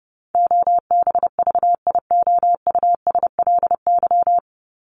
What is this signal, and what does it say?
Morse code, heard as a keyed tone, about 30 wpm, 700 Hz. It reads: OBVIOUSLY
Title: Top 1501-1750 Words 30wpm